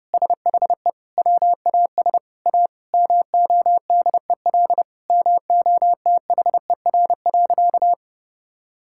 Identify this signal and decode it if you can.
Morse code, heard as a keyed tone, about 30 wpm, 700 Hz. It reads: SHE WAS A MODEL MOTHER.
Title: Sentences from Top 1000 Words - Review Part 5 - 30wpm